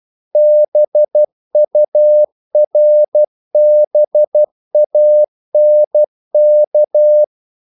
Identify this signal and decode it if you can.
Morse code, heard as a keyed tone, about 12 wpm, 600 Hz. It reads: BURBANK
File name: CW / Practice / EnglishQuotations0312